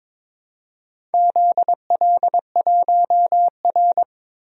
Morse code, heard as a keyed tone, about 22 wpm, 700 Hz. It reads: ZL1R
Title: DXCC-100 Call Signs - 4 Characters - 22wpm